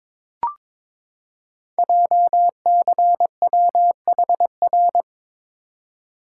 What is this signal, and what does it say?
Morse code, heard as a keyed tone, about 22 wpm, 700 Hz. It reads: JCWHR
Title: Code Groups - 5 Letters - 22wpm